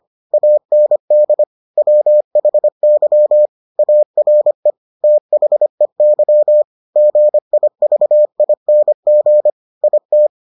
Morse code, heard as a keyed tone, about 25 wpm, 600 Hz. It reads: AND WHY ARE THEY GIVING IT